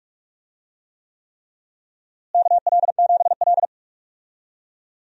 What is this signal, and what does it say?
Morse code, heard as a keyed tone, about 46 wpm, 700 Hz. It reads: KL7L
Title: DXCC-100 Call Signs - 4 Characters - Repeated 3x - 45wpm